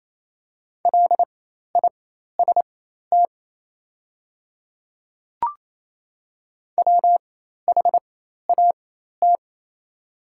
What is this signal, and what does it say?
Morse code, heard as a keyed tone, about 28 wpm, spaced as for 13 wpm, 700 Hz. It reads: LIST WHAT
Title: 4x Character Spacing - Top 500 Words - 28wpm